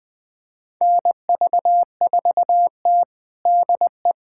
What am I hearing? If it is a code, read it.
Morse code, heard as a keyed tone, about 20 wpm, 700 Hz. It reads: NV4T DE
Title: USA Callsign DE Callsign - 4 Characters - 20wpm